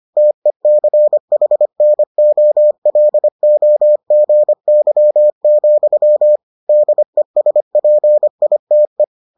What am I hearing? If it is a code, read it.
Morse code, heard as a keyed tone, about 25 wpm, 600 Hz. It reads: TECHNOLOGY, DESPITE